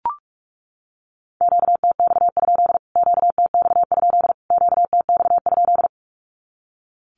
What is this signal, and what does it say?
Morse code, heard as a keyed tone, about 44 wpm, 700 Hz. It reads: QTX? QTX? QTX?
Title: Q-Codes Repeated 3x 45wpm